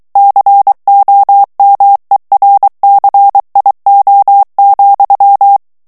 Morse code, heard as a keyed tone, about 23 wpm, 800 Hz. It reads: COMERCIO,
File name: morse mp3 / ESPAGNOL TEXTE EN CLAIR 2 VITESSE 22 MOTS MINUTES